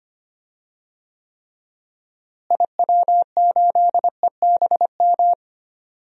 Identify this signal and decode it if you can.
Morse code, heard as a keyed tone, about 25 wpm, 700 Hz. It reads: IW8EBM
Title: DXCC-100 Call Signs - 6 Characters - Repeated 3x - 25wpm